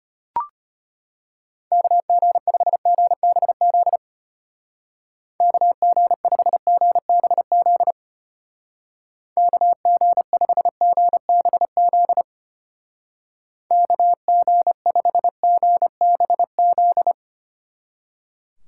Morse code, heard as a keyed tone, about 33 wpm, 700 Hz. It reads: KG5GBZ KG5GBZ KG5GBZ KG5GBZ